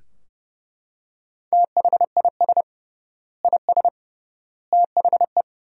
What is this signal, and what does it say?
Morse code, heard as a keyed tone, about 30 wpm, 700 Hz. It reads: THIS IS THE
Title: 3x Word-Spacing - Sentences from Top 200 Words - 30wpm